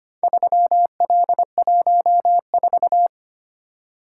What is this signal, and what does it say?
Morse code, heard as a keyed tone, about 25 wpm, 700 Hz. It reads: 3L14
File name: practice / 25wpm / Four Letter-Number Combinations - 25wpm